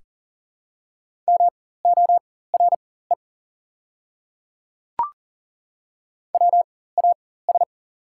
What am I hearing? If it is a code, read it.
Morse code, heard as a keyed tone, about 40 wpm, spaced as for 19 wpm, 690 Hz. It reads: MORE WAS
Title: 4x Character Spacing - Top 300 Words - 40wpm